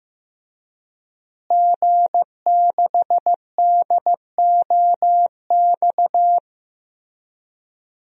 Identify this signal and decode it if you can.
Morse code, heard as a keyed tone, about 14 wpm, 700 Hz. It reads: G6DOX
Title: DXCC-100 Call Signs - 5 Characters - Repeated 3x - 15wpm